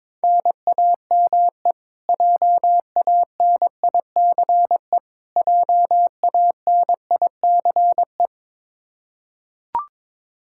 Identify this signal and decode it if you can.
Morse code, heard as a keyed tone, about 22 wpm, 700 Hz. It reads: NAME JANICE JANICE
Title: Practice QSO - Exchange 1 - Beginning - 22wpm